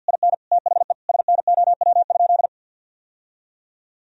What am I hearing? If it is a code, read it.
Morse code, heard as a keyed tone, about 50 wpm, 700 Hz. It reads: IN THE SNOW?
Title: Sentences from Top 1000 Words - Review Part 3 - 50wpm